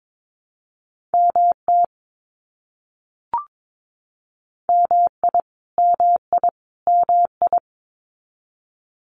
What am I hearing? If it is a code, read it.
Morse code, heard as a keyed tone, about 22 wpm, 700 Hz. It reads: MT MI MI MI
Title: US States Repeated 3x 22wpm